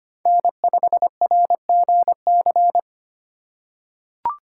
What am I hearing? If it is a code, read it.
Morse code, heard as a keyed tone, about 25 wpm, 700 Hz. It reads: N5RGC
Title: USA Call Signs - 5 Characters - 2x Character Spacing - 25wpm